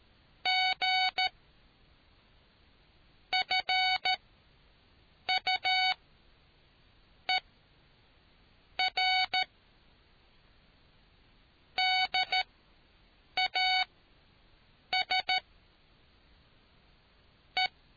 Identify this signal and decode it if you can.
Morse code, heard as a keyed tone, about 13 wpm, spaced as for 6 wpm, 750 Hz. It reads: G FUER DAS E